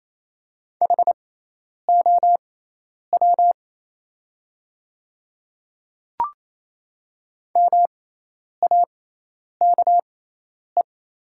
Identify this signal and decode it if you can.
Morse code, heard as a keyed tone, about 28 wpm, spaced as for 10 wpm, 700 Hz. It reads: HOW MAKE
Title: 6x Character Spacing - Top 100 Words - 28wpm